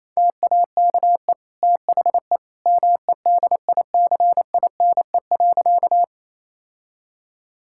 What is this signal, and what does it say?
Morse code, heard as a keyed tone, about 28 wpm, 700 Hz. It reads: TAKE THE MEDICINE.
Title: Sentences from Top 1000 Words - Review Part 6 - 28wpm